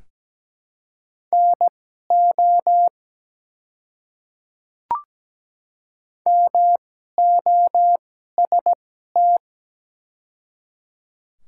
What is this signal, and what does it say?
Morse code, heard as a keyed tone, about 17 wpm, spaced as for 12 wpm, 690 Hz. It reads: NO MOST
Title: 2x Character Spacing - Top 100 Words - 17wpm